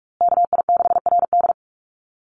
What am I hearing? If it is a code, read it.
Morse code, heard as a keyed tone, about 44 wpm, 700 Hz. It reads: KI6RD
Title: USA Call Signs Repeated 2x 45wpm